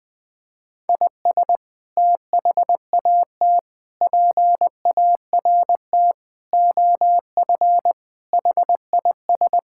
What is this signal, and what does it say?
Morse code, heard as a keyed tone, about 20 wpm, 700 Hz. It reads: IS THAT PART OF HIS
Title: Sentences from Top 1000 Words - Review Part 5 - 20wpm